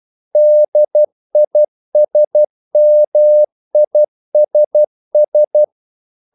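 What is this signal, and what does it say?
Morse code, heard as a keyed tone, about 12 wpm, 600 Hz. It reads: DISMISS
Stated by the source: Morse practice sending English quotations.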